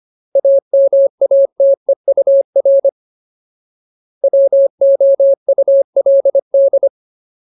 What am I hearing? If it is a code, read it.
Morse code, heard as a keyed tone, about 25 wpm, 550 Hz. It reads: AMATEUR WOULD
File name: Ultimate Word List / 25WPM Ultimate